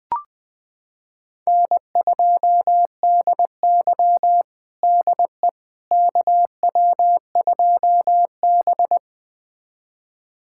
Morse code, heard as a keyed tone, about 20 wpm, 700 Hz. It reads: N2DY DE KW2B
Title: USA Callsign DE Callsign - 4 Characters - 20wpm